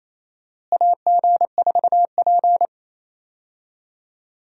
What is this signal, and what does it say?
Morse code, heard as a keyed tone, about 28 wpm, 700 Hz. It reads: AG4P
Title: USA Call Signs - 4 Characters - Repeated 3x - 28wpm